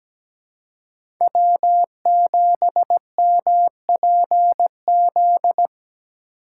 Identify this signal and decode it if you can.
Morse code, heard as a keyed tone, about 17 wpm, 700 Hz. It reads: W7MPZ
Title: USA Call Signs - 5 Characters - 2x Character Spacing - 17wpm